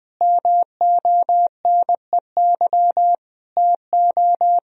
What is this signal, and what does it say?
Morse code, heard as a keyed tone, about 20 wpm, 700 Hz. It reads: MONEY TO
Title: Sentences from Top 1000 Words - Review Part 2 - 20wpm